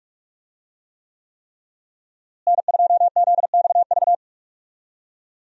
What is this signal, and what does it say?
Morse code, heard as a keyed tone, about 47 wpm, 700 Hz. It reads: N2ZXV
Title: USA Call Signs - 5 Characters - Repeated 3x - 45wpm